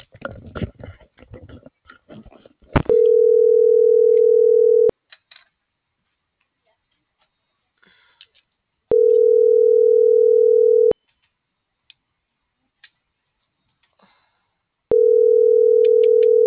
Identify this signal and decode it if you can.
Ringback tone (North America), 440 + 480 Hz.
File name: CENTRAL TERRESTRE / out vane.lazaro